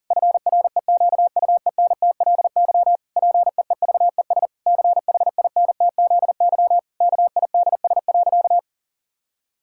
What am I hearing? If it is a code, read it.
Morse code, heard as a keyed tone, about 37 wpm, 700 Hz. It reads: FREQUENTLY PEEVES CHINTZY KIDS.
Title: Pangram Challenge v2 - 40wpm